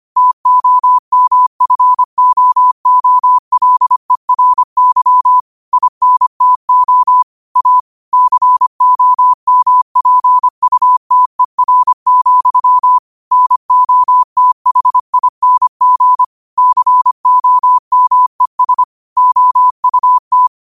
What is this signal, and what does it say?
Morse code, heard as a keyed tone, about 25 wpm, 1000 Hz. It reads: TOMFOOLERY INTO A COMPUTER, NOTHING COMES OUT